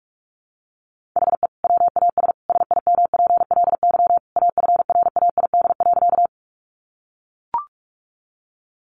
Morse code, heard as a keyed tone, about 44 wpm, 700 Hz. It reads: HE WAS SIMPLY AFRAID.
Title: Sentences from Top 1500 Words 45wpm